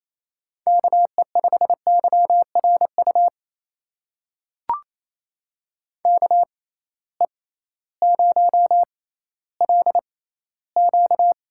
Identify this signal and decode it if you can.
Morse code, heard as a keyed tone, about 28 wpm, spaced as for 15 wpm, 700 Hz. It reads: KE5YRU KE0LQ
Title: USA Call Signs - 6x Character Spacing - 28wpm